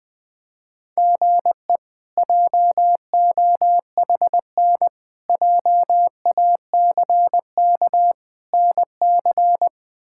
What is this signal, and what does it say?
Morse code, heard as a keyed tone, about 21 wpm, 700 Hz. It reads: GE JOHN JACK NC
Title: K1USN SST QSO Practice - 20wpm